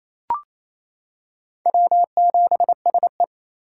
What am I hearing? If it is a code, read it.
Morse code, heard as a keyed tone, about 28 wpm, 700 Hz. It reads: W7SE
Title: K1USN SST QSO Practice - 28wpm